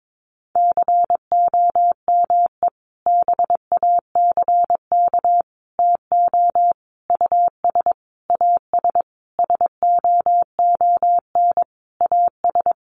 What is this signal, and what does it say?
Morse code, heard as a keyed tone, about 22 wpm, 700 Hz. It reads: COME BACK TO US AS SOON AS
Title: Sentences from Top 500 Words 22wpm